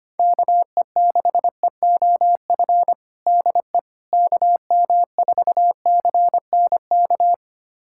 Morse code, heard as a keyed tone, about 25 wpm, 700 Hz. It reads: KE6EOF DE KM4CNK
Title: USA Callsign DE Callsign - 6 Characters - 25wpm